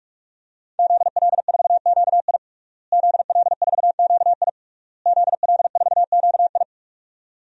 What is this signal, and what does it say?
Morse code, heard as a keyed tone, about 41 wpm, 700 Hz. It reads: ZL4QI ZL4QI ZL4QI
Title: DXCC-100 Call Signs - 5 Characters - Repeated 3x - 45wpm